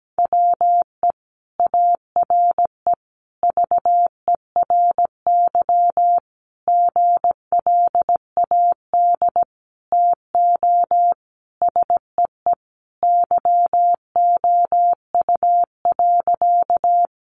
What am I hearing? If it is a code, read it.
Morse code, heard as a keyed tone, about 17 wpm, 700 Hz. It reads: WE ARE VERY GLAD TO SEE YOU.
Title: Sentences from Top 800 Words 17wpm